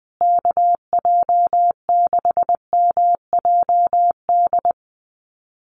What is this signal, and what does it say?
Morse code, heard as a keyed tone, about 20 wpm, 700 Hz. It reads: KJ6MJD
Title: USA Call Signs Repeated 3x 20wpm